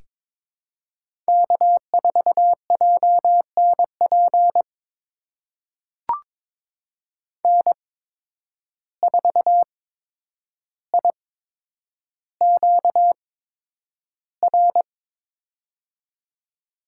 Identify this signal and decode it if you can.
Morse code, heard as a keyed tone, about 21 wpm, spaced as for 9 wpm, 700 Hz. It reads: K4JNP N4IQR
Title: USA Call Signs - 8x Character Spacing - 22wpm